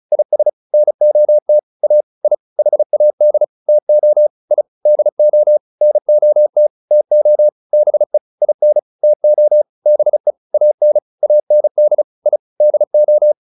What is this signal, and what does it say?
Morse code, heard as a keyed tone, about 34 wpm, 600 Hz. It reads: IS NOT A I HAD TO I DO NOT TO BE IN TO BE AN AND I DO
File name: WD201 35